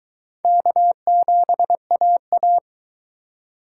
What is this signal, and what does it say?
Morse code, heard as a keyed tone, about 22 wpm, 700 Hz. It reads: K7AA